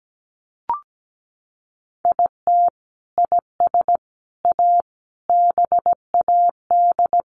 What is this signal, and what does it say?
Morse code, heard as a keyed tone, about 17 wpm, 700 Hz. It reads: IT IS A BAD
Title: Sentences from Top 1000 Words 17wpm